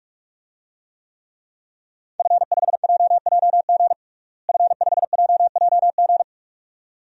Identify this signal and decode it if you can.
Morse code, heard as a keyed tone, about 47 wpm, 700 Hz. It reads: F5JJG F5JJG
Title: DXCC-100 Call Signs - 5 Characters - Repeated 2x - 45wpm